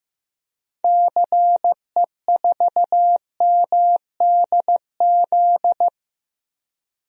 Morse code, heard as a keyed tone, about 15 wpm, 700 Hz. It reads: CE4MDZ
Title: DXCC-100 Call Signs - 6 Characters - Repeated 3x - 15wpm